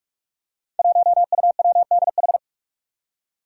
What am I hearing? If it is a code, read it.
Morse code, heard as a keyed tone, about 46 wpm, 700 Hz. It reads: 1UWDH
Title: Five Challenging Letter-Number Combinations 45wpm